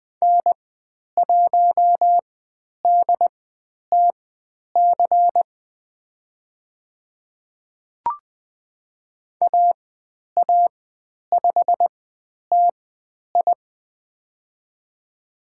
Morse code, heard as a keyed tone, about 20 wpm, spaced as for 10 wpm, 700 Hz. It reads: N1DTC AA5TI
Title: 102 Lesson 17g - Letter u - Call Signs 20wpm